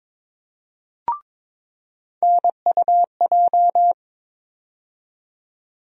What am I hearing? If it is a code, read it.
Morse code, heard as a keyed tone, about 22 wpm, 700 Hz. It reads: NUJ
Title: USA Call Sign Postfixes - 3 Letters 22wpm